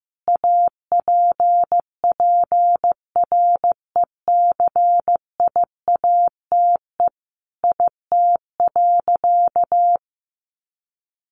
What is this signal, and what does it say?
Morse code, heard as a keyed tone, about 15 wpm, 700 Hz. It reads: APPRECIATE IT.
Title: Sentences from Top 1300 Words 15wpm